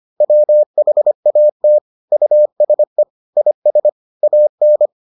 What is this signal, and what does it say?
Morse code, heard as a keyed tone, about 25 wpm, 600 Hz. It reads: WHAT USE IS AN